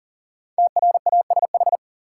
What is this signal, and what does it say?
Morse code, heard as a keyed tone, about 40 wpm, 700 Hz. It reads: TRASH
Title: Five Letter Words - 40wpm